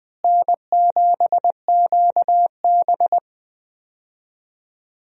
Morse code, heard as a keyed tone, about 20 wpm, 700 Hz. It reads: N7QB
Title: USA Call Signs - 4 Characters - Repeated 3x - 20wpm